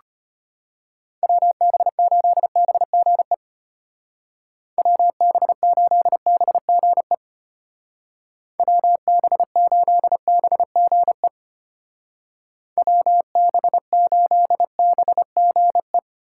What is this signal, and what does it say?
Morse code, heard as a keyed tone, about 30 wpm, 700 Hz. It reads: WB8BGE WB8BGE WB8BGE WB8BGE
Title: Speed Racing - USA 2x3 Call Signs 25wpm